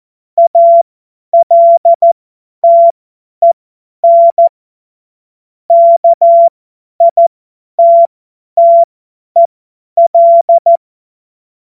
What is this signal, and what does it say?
Morse code, heard as a keyed tone, about 14 wpm, spaced as for 10 wpm, 680 Hz. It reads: ALTEN KITTEL